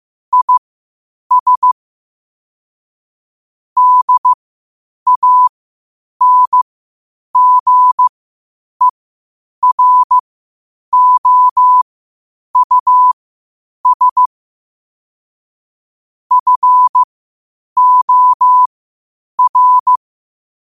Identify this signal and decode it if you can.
Morse code, heard as a keyed tone, about 15 wpm, spaced as for 8 wpm, 1000 Hz. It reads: IS DANGEROUS FOR